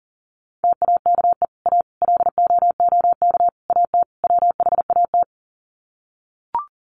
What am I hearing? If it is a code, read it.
Morse code, heard as a keyed tone, about 40 wpm, 700 Hz. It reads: TAKE A LOOK AT WHAT
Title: Sets of 5 Words Encore2 40wpm